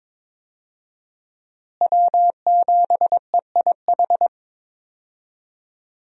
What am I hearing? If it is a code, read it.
Morse code, heard as a keyed tone, about 22 wpm, 700 Hz. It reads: W7EIH
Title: USA Call Signs - 5 Characters - Repeated 2x - 22wpm